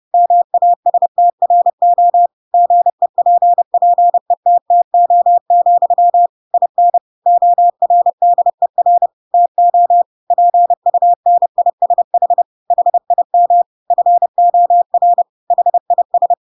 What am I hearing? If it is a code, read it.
Morse code, heard as a keyed tone, about 30 wpm, 700 Hz. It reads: MASTRO GEPPETTO, IN ORDER TO PUNISH HIM FOR HIS
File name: Pinocchio / Pinocchio ch8 30 30 ik1hge